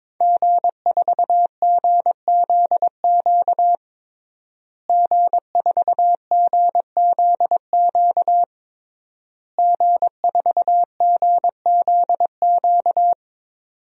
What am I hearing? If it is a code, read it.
Morse code, heard as a keyed tone, about 22 wpm, 700 Hz. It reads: G4GZQ G4GZQ G4GZQ
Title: DXCC-100 Call Signs - 5 Characters - Repeated 3x - 22wpm